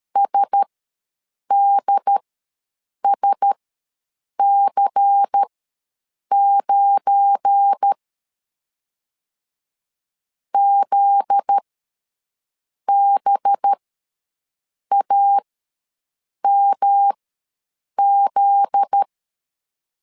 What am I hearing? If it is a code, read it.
Morse code, heard as a keyed tone, about 13 wpm, spaced as for 6 wpm, 800 Hz. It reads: SDSC9 ZBAMZ
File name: LEZIONI / 7 GF9Z / 7 GF9Z ws 12wpm 3321